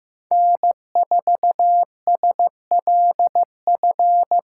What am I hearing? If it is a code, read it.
Morse code, heard as a keyed tone, about 18 wpm, 700 Hz. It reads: N4SLF
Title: Speed Racing - USA 1x3 Call Signs 15wpm